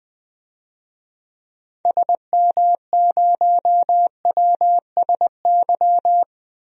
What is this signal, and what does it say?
Morse code, heard as a keyed tone, about 20 wpm, 700 Hz. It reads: SM0WSY
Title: DXCC-100 Call Signs - 6 Characters - 20wpm